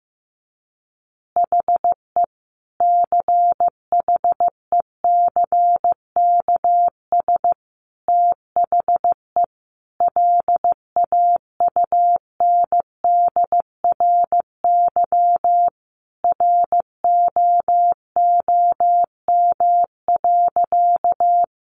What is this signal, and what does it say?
Morse code, heard as a keyed tone, about 15 wpm, 700 Hz. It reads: HE CHECKS THE LAUNDRY ROOM.
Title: Easy Phrases 15wpm